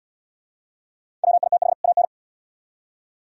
Morse code, heard as a keyed tone, about 65 wpm, 700 Hz. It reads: WITH AN